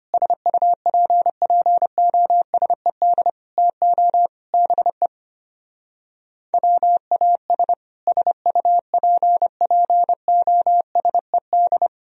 Morse code, heard as a keyed tone, about 31 wpm, 700 Hz. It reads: SUPPOSED TO BE WAS SUPPOSED